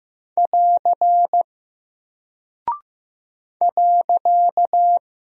Morse code, heard as a keyed tone, about 15 wpm, 700 Hz. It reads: <AR> .
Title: Single Character (Letter:Number:Punctuation:Prosign) 15wpm